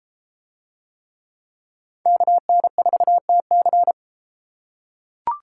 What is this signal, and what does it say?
Morse code, heard as a keyed tone, about 31 wpm, 700 Hz. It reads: KN4TC